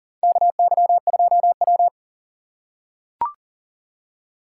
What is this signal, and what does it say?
Morse code, heard as a keyed tone, about 40 wpm, 700 Hz. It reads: KY2W
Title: USA Call Signs - 2x Character Spacing - 40wpm